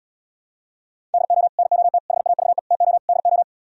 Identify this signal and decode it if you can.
Morse code, heard as a keyed tone, about 77 wpm, 690 Hz. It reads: DO NOT HAVE TO DO